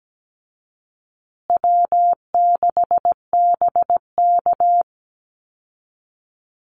Morse code, heard as a keyed tone, about 17 wpm, 700 Hz. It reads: W6BK
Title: Easy Call Signs Repeated 3x - 17wpm